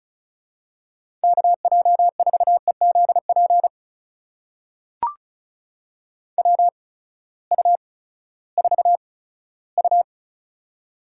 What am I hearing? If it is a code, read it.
Morse code, heard as a keyed tone, about 34 wpm, spaced as for 17 wpm, 700 Hz. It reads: KJ4EZP WU4U
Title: USA Call Signs - 8x Character Spacing - 35wpm